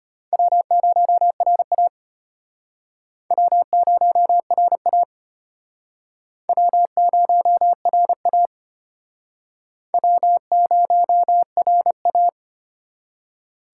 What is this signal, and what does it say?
Morse code, heard as a keyed tone, about 34 wpm, 700 Hz. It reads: W0RA W0RA W0RA W0RA